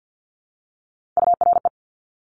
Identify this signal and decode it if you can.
Morse code, heard as a keyed tone, about 50 wpm, 700 Hz. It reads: URE
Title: Top Three Letter Combinations in Words 50wpm